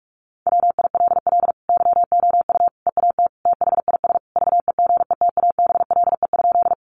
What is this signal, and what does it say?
Morse code, heard as a keyed tone, about 44 wpm, 700 Hz. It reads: WILL YOU EAT THIS VEGETABLE?
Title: Sentences from Top 1300 Words 45wpm